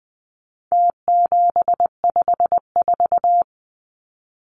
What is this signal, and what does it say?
Morse code, heard as a keyed tone, about 20 wpm, 700 Hz. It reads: T754
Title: Numbers - Contest Style - 4-Digit 20wpm